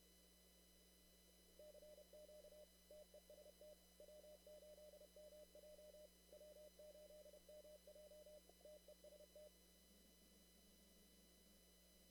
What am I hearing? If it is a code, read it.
Morse code, heard as a keyed tone, about 31 wpm, 600 Hz, with noise. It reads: CQ TEST W8MJ W8MJ TEST
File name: CW / kt5j wpx cw-op-k5pi